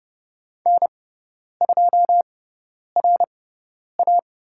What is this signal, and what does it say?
Morse code, heard as a keyed tone, about 30 wpm, spaced as for 10 wpm, 700 Hz. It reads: N2RA
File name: Lesson 16g - Letter c - Call Signs 30wpm